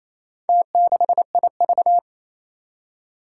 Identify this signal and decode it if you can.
Morse code, heard as a keyed tone, about 28 wpm, 700 Hz. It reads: T6IV
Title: Four Challenging Letter-Number Combinations 28wpm